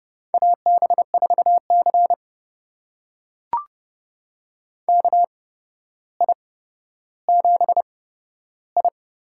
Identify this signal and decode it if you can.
Morse code, heard as a keyed tone, about 30 wpm, spaced as for 13 wpm, 700 Hz. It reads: AB4C KI7I
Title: USA Call Signs - 8x Character Spacing - 30wpm